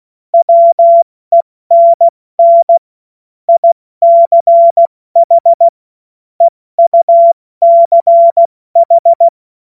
Morse code, heard as a keyed tone, about 16 wpm, 680 Hz. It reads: WENN ICH EUCH
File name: Morsekurs / Witze / Witz 20. Januar 2026 70 F80